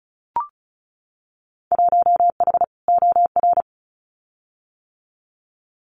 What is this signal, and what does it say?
Morse code, heard as a keyed tone, about 34 wpm, 700 Hz. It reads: 1H OR
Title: Winter Field Day Exchanges 35wpm